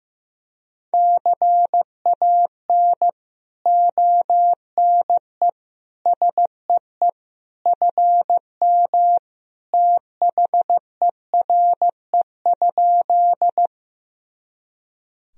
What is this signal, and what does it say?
Morse code, heard as a keyed tone, about 15 wpm, 700 Hz. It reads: CAN ONE SEE FM THERE?
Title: Sentences from Top 100 Words and Abbreviations 15wpm